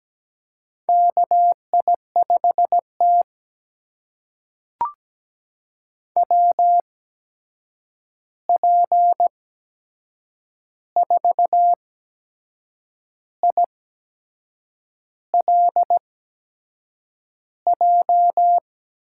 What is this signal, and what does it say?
Morse code, heard as a keyed tone, about 17 wpm, spaced as for 6 wpm, 700 Hz. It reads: KI5T WP4ILJ
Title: USA Call Signs - 8x Character Spacing - 17wpm